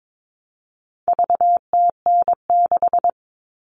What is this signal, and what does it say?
Morse code, heard as a keyed tone, about 22 wpm, 700 Hz. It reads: VTN6
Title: All Cut Numbers - 4-Digit 22wpm